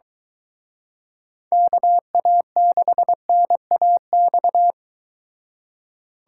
Morse code, heard as a keyed tone, about 22 wpm, 700 Hz. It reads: KA6NAX